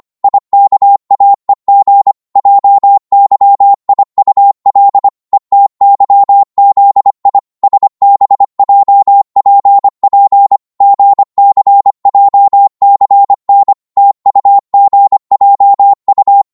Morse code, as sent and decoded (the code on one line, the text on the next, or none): .. -.- .- . --. / .--- -.-- .. ..- .-.. / . - -.-- --.. .. / ... -... .--- .--. .--. / --. -.-. .--- -.-. -. / - ..- --. .--- ..-
IKAEG JYIUL ETYZI SBJPP GCJCN TUGJU